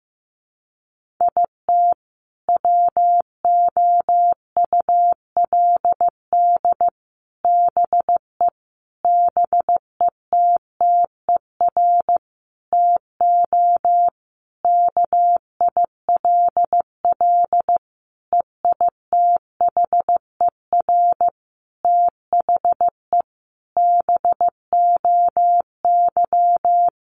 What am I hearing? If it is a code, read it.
Morse code, heard as a keyed tone, about 15 wpm, 700 Hz. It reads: IT WOULD BE BETTER TO KILL EITHER THE BOY
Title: Sentences from Top 1500 Words 15wpm